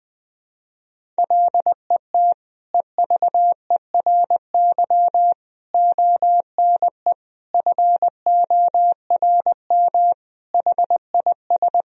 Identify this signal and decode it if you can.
Morse code, heard as a keyed tone, about 19 wpm, 700 Hz. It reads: LET EVERY ONE FORM HIS
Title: Sentences from Top 1000 Words - Review Part 1 - 20wpm